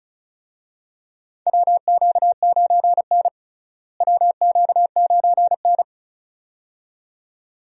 Morse code, heard as a keyed tone, about 34 wpm, 700 Hz. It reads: WQ9N WQ9N
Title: USA Call Signs - 4 Characters - Repeated 2x - 35wpm